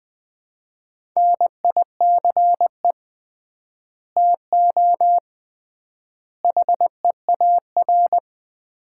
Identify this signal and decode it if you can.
Morse code, heard as a keyed tone, about 20 wpm, 700 Hz. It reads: NICE TO HEAR
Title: Common QSO Phrases - 3x Word-Spacing - 20wpm